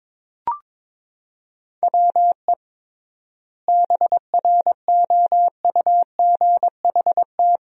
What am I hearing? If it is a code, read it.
Morse code, heard as a keyed tone, about 22 wpm, 700 Hz. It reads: WE BROUGHT
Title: 3x Word-Spacing - Sentences from Top 500 Words - 22wpm